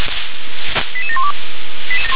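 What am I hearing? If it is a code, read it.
Five-tone selective call (ZVEI), heard as quick runs of five tones, tones of about 70 ms. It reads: 88712
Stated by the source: ZVEI five-tone selective call received on an SDR.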